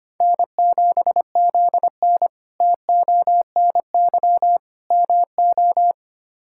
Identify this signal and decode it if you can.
Morse code, heard as a keyed tone, about 25 wpm, 700 Hz. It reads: N7ZN TONY MO
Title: K1USN SST QSO Practice - 25wpm